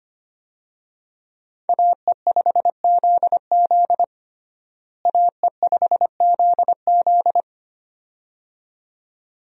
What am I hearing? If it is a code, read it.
Morse code, heard as a keyed tone, about 25 wpm, 700 Hz. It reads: AE5ZZ AE5ZZ
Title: USA Call Signs - 5 Characters - Repeated 2x - 25wpm